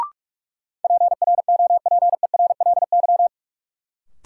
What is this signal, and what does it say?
Morse code, heard as a keyed tone, about 46 wpm, 700 Hz. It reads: PROPERLY
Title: Sentences from Top 1100 Words - New Word Drill - 45wpm